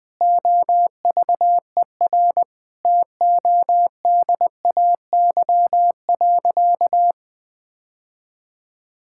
Morse code, heard as a keyed tone, about 20 wpm, 700 Hz. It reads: OVER TODAY.
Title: Sentences with 4-Word Phrases - 20wpm